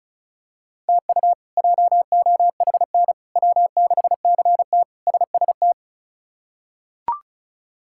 Morse code, heard as a keyed tone, about 36 wpm, 700 Hz. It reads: TU JOHN W6CT SST
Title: K1USN SST QSO Practice - 35wpm